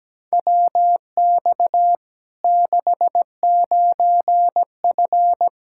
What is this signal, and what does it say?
Morse code, heard as a keyed tone, about 17 wpm, 700 Hz. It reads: WX 69F
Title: Practice QSO - Exchange 2 - Info - 17wpm